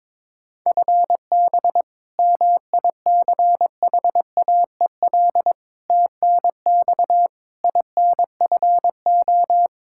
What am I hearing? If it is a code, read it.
Morse code, heard as a keyed tone, about 22 wpm, 700 Hz. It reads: FB MICHAEL TNX INFO
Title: Practice QSO - Exchange 3 - More Info - 22wpm